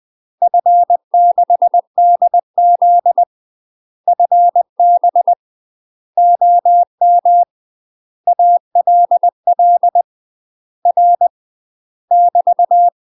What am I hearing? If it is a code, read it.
Morse code, heard as a keyed tone, about 20 wpm, 700 Hz. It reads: F6DZ FB OM ALL R <BT>
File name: UK MASTER QSOs TRAINING / 20WPM / UK MASTER QSOs 0002